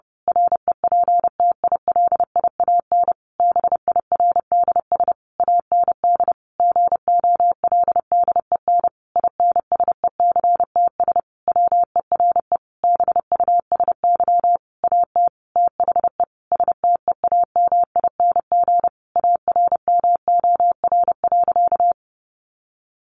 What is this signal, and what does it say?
Morse code, heard as a keyed tone, about 30 wpm, 700 Hz. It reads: REPTILIAN BIRDS AND GOLDEN INSECTS WERE BUSY AT THE STEAMING ARMOR.
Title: A Sound Of Thunder 30wpm